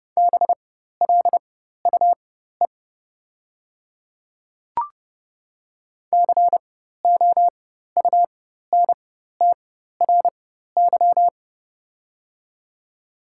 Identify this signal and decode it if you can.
Morse code, heard as a keyed tone, about 29 wpm, spaced as for 14 wpm, 690 Hz. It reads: BLUE COUNTRY
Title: 4x Character Spacing - Top 500 Words - 30wpm